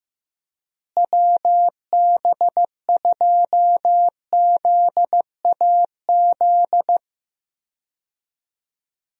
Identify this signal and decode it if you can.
Morse code, heard as a keyed tone, about 15 wpm, 700 Hz. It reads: WB2ZAZ
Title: USA Call Signs - 6 Characters - Repeated 3x - 15wpm